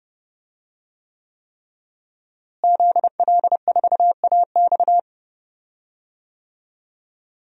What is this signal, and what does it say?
Morse code, heard as a keyed tone, about 30 wpm, 700 Hz. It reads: ZL4AX
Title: DXCC-100 Call Signs - 5 Characters - Repeated 3x - 30wpm